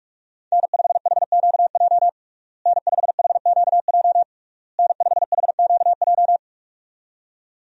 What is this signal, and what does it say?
Morse code, heard as a keyed tone, about 46 wpm, 700 Hz. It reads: N5HQJ N5HQJ N5HQJ
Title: USA Call Signs - 5 Characters - Repeated 3x - 45wpm